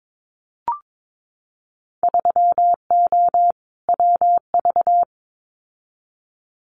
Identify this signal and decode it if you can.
Morse code, heard as a keyed tone, about 22 wpm, 700 Hz. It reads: 3O WV
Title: Winter Field Day Exchanges 22wpm